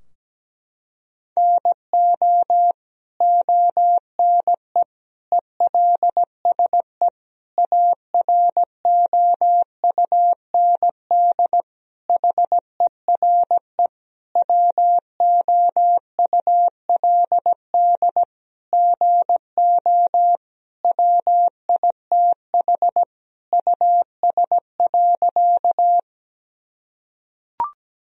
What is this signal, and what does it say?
Morse code, heard as a keyed tone, about 17 wpm, 700 Hz. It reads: NO ONE ELSE AROUND HERE WOULD GO WITH US.
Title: Sentences from Top 1000 Words - Review Part 3 - 17wpm